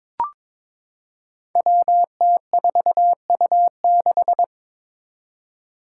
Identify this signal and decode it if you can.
Morse code, heard as a keyed tone, about 22 wpm, 700 Hz. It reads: WT4U6
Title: Five Challenging Letter-Number Combinations 22wpm